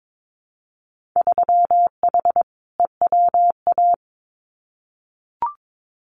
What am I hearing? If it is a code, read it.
Morse code, heard as a keyed tone, about 22 wpm, 700 Hz. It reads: 3H EWA
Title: Winter Field Day Exchanges Encore 22wpm